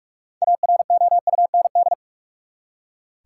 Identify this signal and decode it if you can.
Morse code, heard as a keyed tone, about 46 wpm, 700 Hz. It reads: AROUND
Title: Six Letter Words - 45wpm